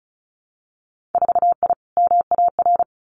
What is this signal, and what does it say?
Morse code, heard as a keyed tone, about 36 wpm, 700 Hz. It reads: 4I MAR
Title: Winter Field Day Exchanges Encore 35wpm